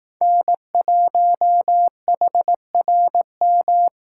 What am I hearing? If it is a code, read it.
Morse code, heard as a keyed tone, about 19 wpm, 700 Hz. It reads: N1HRM
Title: Speed Racing - USA 1x3 Call Signs 15wpm